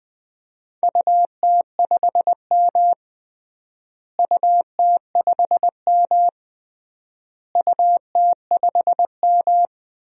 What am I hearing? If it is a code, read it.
Morse code, heard as a keyed tone, about 21 wpm, 700 Hz. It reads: UT5M UT5M UT5M
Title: DXCC-100 Call Signs - 4 Characters - Repeated 3x - 20wpm